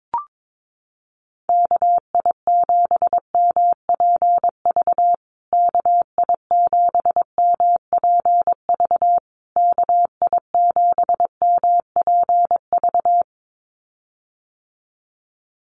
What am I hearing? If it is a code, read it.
Morse code, heard as a keyed tone, about 22 wpm, 700 Hz. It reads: KI7MPV KI7MPV KI7MPV
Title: USA Call Signs Repeated 3x 22wpm